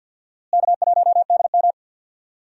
Morse code, heard as a keyed tone, about 49 wpm, 700 Hz. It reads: K1DM
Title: USA Call Signs - 4 Characters - Repeated 2x - 50wpm